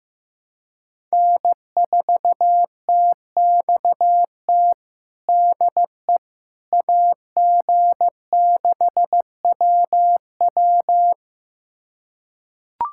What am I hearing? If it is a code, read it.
Morse code, heard as a keyed tone, about 15 wpm, 700 Hz. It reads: N4TXT DE AG6WW
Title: USA Callsign DE Callsign - 5 Characters - 15wpm